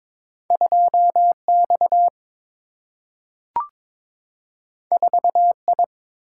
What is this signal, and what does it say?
Morse code, heard as a keyed tone, about 21 wpm, 700 Hz. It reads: 2X 4I
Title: Single-Letter Single-Number 22wpm